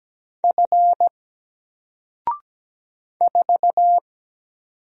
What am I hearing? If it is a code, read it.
Morse code, heard as a keyed tone, about 17 wpm, 700 Hz. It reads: F 4
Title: Single Letter-Number Alt 17wpm